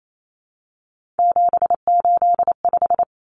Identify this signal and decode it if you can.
Morse code, heard as a keyed tone, about 28 wpm, 700 Hz. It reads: 785
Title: Numbers - 3-Digit 28wpm